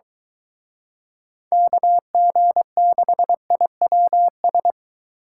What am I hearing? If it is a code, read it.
Morse code, heard as a keyed tone, about 23 wpm, 700 Hz. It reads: KG6IWS